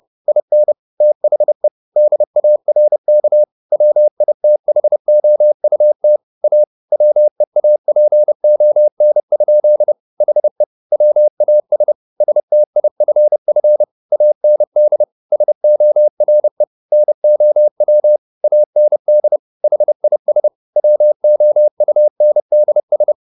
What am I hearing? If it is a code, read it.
Morse code, heard as a keyed tone, about 30 wpm, 600 Hz. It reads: IN THE DARK WITHOUT A WEAPON? HE WAS STIFF AND SORE NOW AND HIS WOUNDS